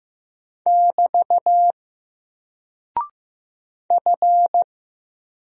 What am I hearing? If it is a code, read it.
Morse code, heard as a keyed tone, about 15 wpm, 700 Hz. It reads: <BT> F
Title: Single Character (Letter:Number:Punctuation:Prosign) Alt 15wpm